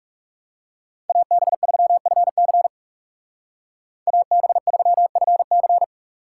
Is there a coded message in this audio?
Yes: Morse code from 1.1 to 5.8 s, heard as a keyed tone, about 44 wpm, 700 Hz. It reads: AB3FC AB3FC